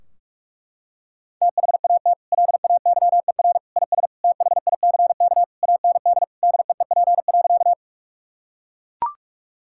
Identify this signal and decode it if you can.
Morse code, heard as a keyed tone, about 41 wpm, 700 Hz. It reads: THAT LAYER IS THICK AND DEEP.
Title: Sentences from Top 1600 Words 45wpm